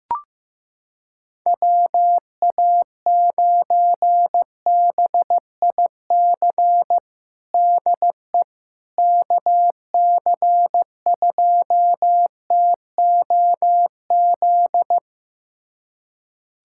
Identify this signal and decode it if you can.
Morse code, heard as a keyed tone, about 15 wpm, 700 Hz. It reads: WA9BIC DE KC2TOZ
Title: USA Callsign DE Callsign 15wpm